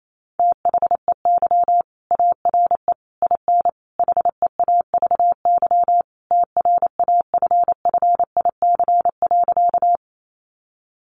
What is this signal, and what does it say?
Morse code, heard as a keyed tone, about 28 wpm, 700 Hz. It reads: THEY ARE IN HEAVY TRAFFIC.
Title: Sentences from Top 1300 Words 28wpm